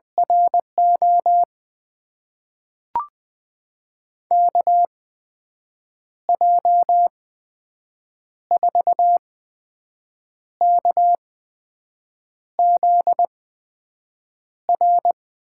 Morse code, heard as a keyed tone, about 20 wpm, spaced as for 6 wpm, 700 Hz. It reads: RO KJ4KZR
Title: USA Call Signs - 6 Characters - 8x Character Spacing - 20wpm